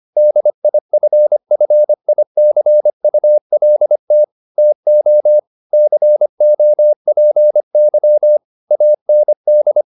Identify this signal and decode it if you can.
Morse code, heard as a keyed tone, about 25 wpm, 600 Hz. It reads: DIFFICULT TO COPY AND